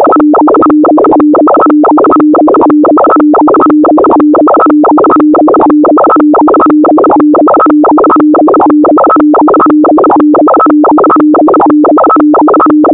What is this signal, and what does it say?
Two-tone teleprinter signal, about 200 Hz shift.
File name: audio notizie / radia s24 n297 radiohelsinki raggam dataradio pt4